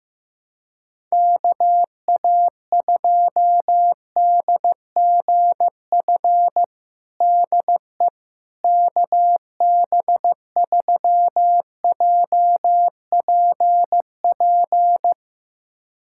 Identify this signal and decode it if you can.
Morse code, heard as a keyed tone, about 15 wpm, 700 Hz. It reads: KA2DGF DE KB3JPP
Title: USA Callsign DE Callsign - 6 Characters - 15wpm